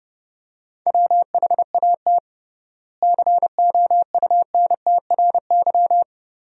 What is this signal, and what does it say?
Morse code, heard as a keyed tone, about 32 wpm, 700 Hz. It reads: WHAT COUNTRY
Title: 3x Word-Spacing - Sentences from Top 300 Words - 30wpm